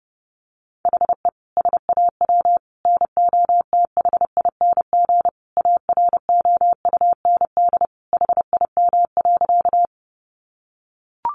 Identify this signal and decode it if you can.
Morse code, heard as a keyed tone, about 30 wpm, 700 Hz. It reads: HE SAW NOTHING AROUND HIM.
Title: Sentences from Top 600 Words 30wpm